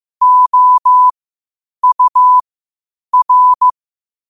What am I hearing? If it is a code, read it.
Morse code, heard as a keyed tone, about 15 wpm, spaced as for 8 wpm, 1000 Hz. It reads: OUR
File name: qotdfiles / 8wpm